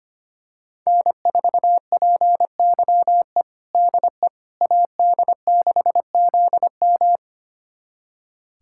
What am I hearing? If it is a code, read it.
Morse code, heard as a keyed tone, about 25 wpm, 700 Hz. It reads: N4PYE DE AD6ZM
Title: USA Callsign DE Callsign - 5 Characters - 25wpm